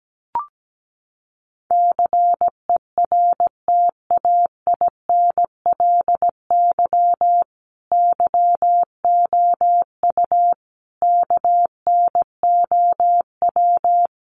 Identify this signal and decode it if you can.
Morse code, heard as a keyed tone, about 17 wpm, 700 Hz. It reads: CERTAINLY YOU KNOW
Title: Sentences from Top 900 Words 17wpm